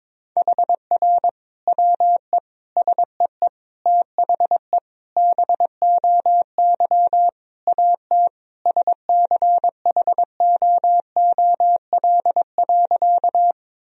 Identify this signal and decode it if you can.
Morse code, heard as a keyed tone, about 21 wpm, 700 Hz. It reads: HR WE SEE THE BOY AT SCHOOL.
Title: Sentences from Top 300 Words and Abbreviations 22wpm